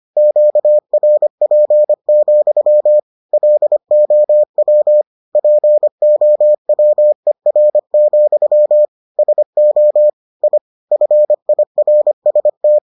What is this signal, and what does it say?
Morse code, heard as a keyed tone, about 25 wpm, 600 Hz. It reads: QRP, LOW POWER, SO I FIRST